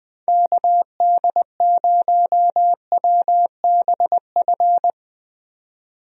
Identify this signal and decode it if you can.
Morse code, heard as a keyed tone, about 20 wpm, 700 Hz. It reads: KD0WBF
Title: USA Call Signs - 6 Characters - Repeated 3x - 20wpm